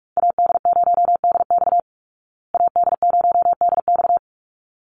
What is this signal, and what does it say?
Morse code, heard as a keyed tone, about 44 wpm, 700 Hz. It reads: AD0DX AD0DX
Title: Call Signs Repeated 3x - 45wpm